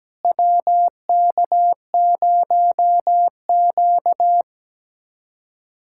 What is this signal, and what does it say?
Morse code, heard as a keyed tone, about 17 wpm, 700 Hz. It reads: WK0Q
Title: USA Callsign DE Callsign - 4 Characters - 17wpm